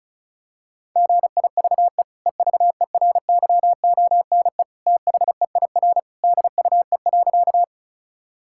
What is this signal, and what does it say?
Morse code, heard as a keyed tone, about 34 wpm, 700 Hz. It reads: GIVE EVERYONE THEIR DUE.
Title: Sentences from Top 1000 Words - Review Part 6 - 35wpm